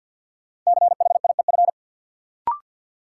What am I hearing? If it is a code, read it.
Morse code, heard as a keyed tone, about 53 wpm, 700 Hz. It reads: CHIEF
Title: Sets of 5 Words Encore - New Word Drill - 50wpm